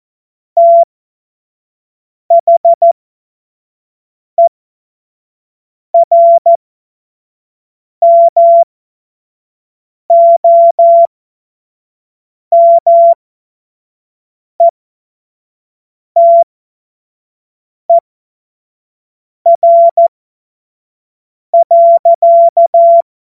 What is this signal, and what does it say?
Morse code, heard as a keyed tone, about 14 wpm, spaced as for 5 wpm, 680 Hz. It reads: THERMOMETER.
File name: Morsekurs / Witze / Witz 16 April 2024 25 F70